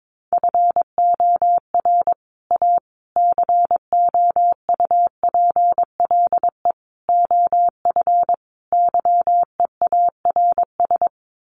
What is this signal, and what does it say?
Morse code, heard as a keyed tone, about 22 wpm, 700 Hz. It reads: FOR A COUPLE OF YEARS
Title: Sets of 5 Words Encore 22wpm